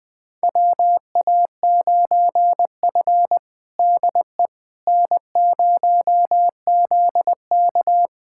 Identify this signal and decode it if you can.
Morse code, heard as a keyed tone, about 21 wpm, 700 Hz. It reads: WA9F DE N0ZK
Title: USA Callsign DE Callsign - 4 Characters - 20wpm